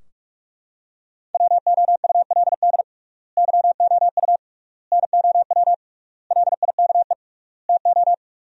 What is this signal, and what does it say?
Morse code, heard as a keyed tone, about 42 wpm, 700 Hz. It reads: WOULD YOU NOW LIKE TO
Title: 3x Word-Spacing - Sentences from Top 200 Words - 45wpm